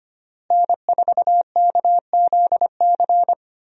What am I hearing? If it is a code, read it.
Morse code, heard as a keyed tone, about 25 wpm, 700 Hz. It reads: N4KZC
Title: USA Call Signs - 5 Characters - Repeated 3x - 25wpm